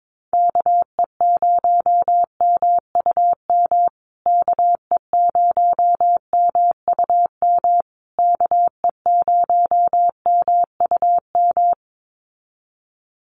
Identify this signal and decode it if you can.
Morse code, heard as a keyed tone, about 22 wpm, 700 Hz. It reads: KE0MUM KE0MUM KE0MUM
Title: USA Call Signs Repeated 3x 22wpm